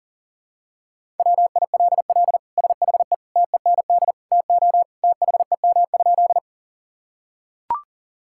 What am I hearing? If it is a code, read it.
Morse code, heard as a keyed tone, about 40 wpm, 700 Hz. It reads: WILL SHE TEND TO THEM?
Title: Sentences from Top 1000 Words - Review Part 5 - 40wpm